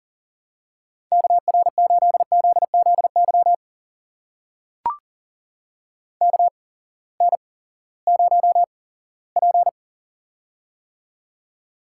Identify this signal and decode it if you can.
Morse code, heard as a keyed tone, about 40 wpm, spaced as for 20 wpm, 700 Hz. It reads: KR8ZZY KN0P
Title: USA Call Signs - 8x Character Spacing - 40wpm